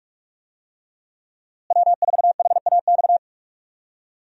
Morse code, heard as a keyed tone, about 46 wpm, 700 Hz. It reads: W4HAX
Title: USA Call Signs - 5 Characters - Repeated 3x - 45wpm